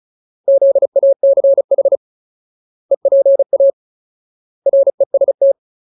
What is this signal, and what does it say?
Morse code, heard as a keyed tone, about 34 wpm, 550 Hz. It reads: ZACH EPA REST
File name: Ultimate Word List / 35WPM Ultimate